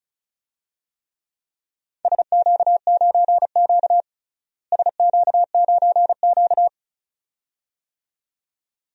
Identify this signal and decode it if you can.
Morse code, heard as a keyed tone, about 34 wpm, 700 Hz. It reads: SQ9Q SQ9Q
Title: DXCC-100 Call Signs - 4 Characters - Repeated 2x - 35wpm